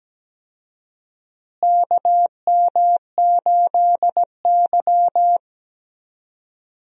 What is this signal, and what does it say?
Morse code, heard as a keyed tone, about 17 wpm, 700 Hz. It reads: KM8Y
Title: USA Call Signs - 4 Characters - Repeated 2x - 17wpm